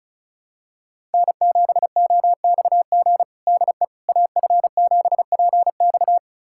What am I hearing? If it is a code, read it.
Morse code, heard as a keyed tone, about 34 wpm, 700 Hz. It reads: N7OXG DE AF7PX
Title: USA Callsign DE Callsign - 5 Characters - 35wpm